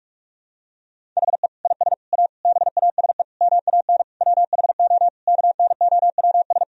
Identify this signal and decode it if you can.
Morse code, heard as a keyed tone, about 42 wpm, 700 Hz. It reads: HE IS A BASE MAN WHO KNOWS
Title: Sentences from Top 1000 Words - Review Part 1 - 45wpm